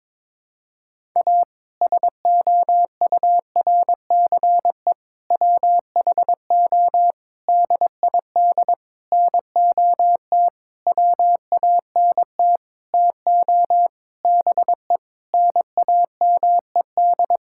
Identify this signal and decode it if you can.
Morse code, heard as a keyed tone, about 22 wpm, 700 Hz. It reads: A SOURCE WHO DID NOT WANT TO BE NAMED
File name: practice / 22wpm / Sentences with 8-Word Phrases - 22wpm